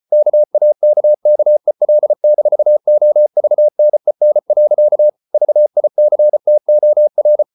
Morse code, heard as a keyed tone, about 34 wpm, 600 Hz. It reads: KAKKEL-OVNEN. VICTOR
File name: Ferie QRQ 34wpm